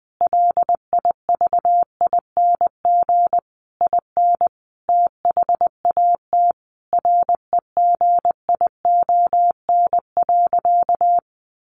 Morse code, heard as a keyed tone, about 20 wpm, 700 Hz. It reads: LIVING IN THAT REGION.
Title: Sentences from Top 1500 Words 20wpm